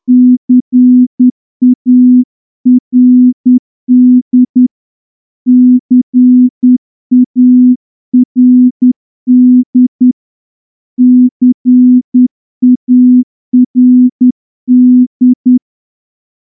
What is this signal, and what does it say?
Morse code, heard as a keyed tone, about 10 wpm, 250 Hz. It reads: CARD CARD CARD
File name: morse-training-10WPM